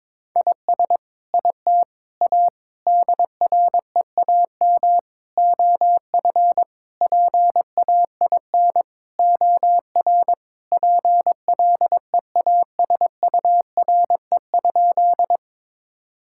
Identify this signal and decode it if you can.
Morse code, heard as a keyed tone, about 22 wpm, 700 Hz. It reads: IS IT A DREAM OF PAIN OR PLEASURE?
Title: Sentences from Top 1600 Words 22wpm